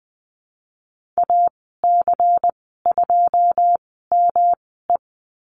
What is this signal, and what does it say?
Morse code, heard as a keyed tone, about 20 wpm, spaced as for 15 wpm, 700 Hz. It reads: AC2ME
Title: 173 Lesson 28g - Number 9 - Call Signs 20wpm